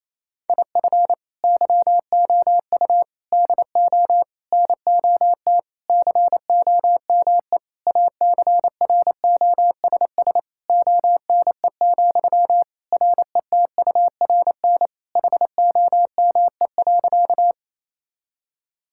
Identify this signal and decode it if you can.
Morse code, heard as a keyed tone, about 28 wpm, 700 Hz. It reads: IF YOU DO NOT COME ACROSS ONE, RETURN HOME.
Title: Sentences from Top 1000 Words - Review Part 5 - 28wpm